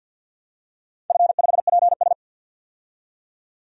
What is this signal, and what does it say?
Morse code, heard as a keyed tone, about 52 wpm, 700 Hz. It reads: F5PS
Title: DXCC-100 Call Signs - 4 Characters - 50wpm